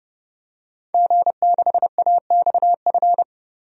Morse code, heard as a keyed tone, about 30 wpm, 700 Hz. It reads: G6AXF
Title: DXCC-100 Call Signs - 5 Characters - 30wpm